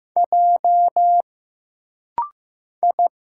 Morse code, heard as a keyed tone, about 15 wpm, 700 Hz. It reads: J I
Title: Single Letters - Rapid-Fire 15wpm